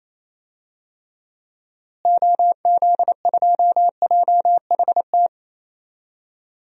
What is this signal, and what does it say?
Morse code, heard as a keyed tone, about 28 wpm, 700 Hz. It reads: OZ2JHT
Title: DXCC-100 Call Signs - 6 Characters - Repeated 2x - 28wpm